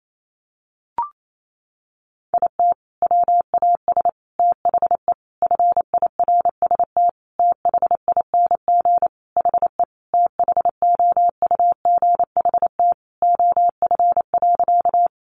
Morse code, heard as a keyed tone, about 28 wpm, 700 Hz. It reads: IT WAS THE FIRST THING HE THOUGHT OF.
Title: Sentences from Top 300 Words 28wpm